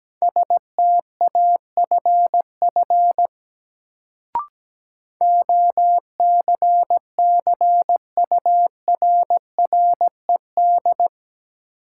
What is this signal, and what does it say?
Morse code, heard as a keyed tone, about 17 wpm, 700 Hz. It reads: STAFF OCCURRED
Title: Sets of 5 Words Encore - New Word Drill - 17wpm